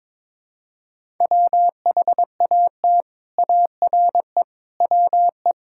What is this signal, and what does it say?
Morse code, heard as a keyed tone, about 22 wpm, 700 Hz. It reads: WHAT ARE WE
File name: practice / 22wpm / Build A Phrase - 5-Word Phrases - 22wpm